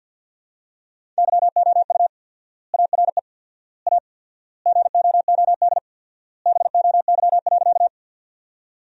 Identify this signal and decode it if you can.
Morse code, heard as a keyed tone, about 50 wpm, 700 Hz. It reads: YOU ARE A GOOD BOY.
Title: 4x Word-Spacing - Sentences from Top 200 Words - 50wpm